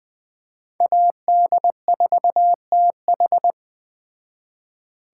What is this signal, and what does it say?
Morse code, heard as a keyed tone, about 20 wpm, 700 Hz. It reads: AD4TH
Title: Speed Racing - USA 2x2 Call Signs 17wpm